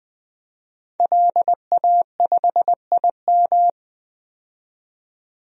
Morse code, heard as a keyed tone, about 20 wpm, 700 Hz. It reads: LA5IM
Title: DXCC-100 Call Signs - 5 Characters - 20wpm